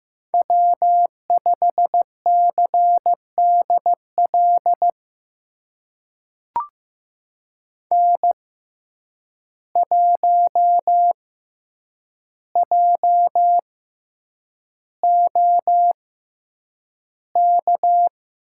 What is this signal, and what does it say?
Morse code, heard as a keyed tone, about 15 wpm, spaced as for 8 wpm, 700 Hz. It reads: W5CDL N1JOK
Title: USA Call Signs - 6x Character Spacing - 15wpm